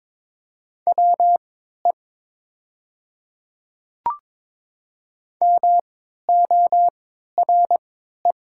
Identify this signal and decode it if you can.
Morse code, heard as a keyed tone, about 21 wpm, spaced as for 12 wpm, 700 Hz. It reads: WE MORE
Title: 3x Character Spacing - Top 100 Words - 22wpm